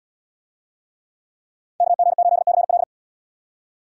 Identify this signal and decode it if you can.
Morse code, heard as a keyed tone, about 71 wpm, 700 Hz. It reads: KG8CL